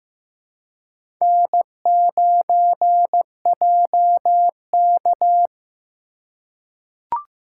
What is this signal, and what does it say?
Morse code, heard as a keyed tone, about 15 wpm, 700 Hz. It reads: N9JK
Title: USA Call Signs - 4 Characters - 4x Character Spacing - 15wpm